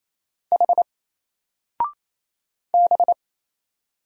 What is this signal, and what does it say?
Morse code, heard as a keyed tone, about 27 wpm, 700 Hz. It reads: H B
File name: Single-Letter-Number-28wpm